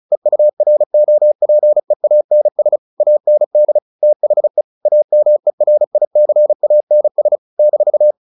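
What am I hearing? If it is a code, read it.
Morse code, heard as a keyed tone, about 34 wpm, 600 Hz. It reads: EUROPEANS AND THE AMERICANS -
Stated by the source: Morse practice at 35 wpm from CW med Gnister, sending Ken Follett's novel Never in English.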